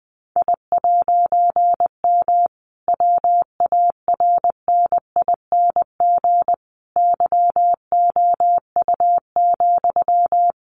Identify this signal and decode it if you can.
Morse code, heard as a keyed tone, about 20 wpm, 700 Hz. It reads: I'M WARNING YOU,
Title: A Sound Of Thunder 20wpm